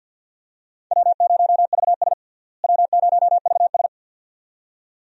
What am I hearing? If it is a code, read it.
Morse code, heard as a keyed tone, about 53 wpm, 700 Hz. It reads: W0VS W0VS
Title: USA Call Signs - 4 Characters - Repeated 2x - 50wpm